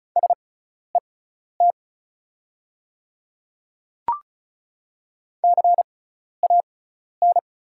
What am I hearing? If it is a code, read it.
Morse code, heard as a keyed tone, about 34 wpm, spaced as for 12 wpm, 690 Hz. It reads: SET CAN
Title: 6x Character Spacing - Top 500 Words - 35wpm